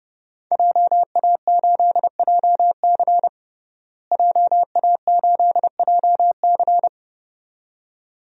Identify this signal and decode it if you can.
Morse code, heard as a keyed tone, about 30 wpm, 700 Hz. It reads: JA8JC JA8JC
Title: DXCC-100 Call Signs - 5 Characters - Repeated 2x - 30wpm